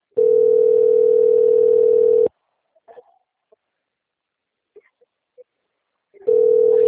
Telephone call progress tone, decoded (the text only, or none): ringback tone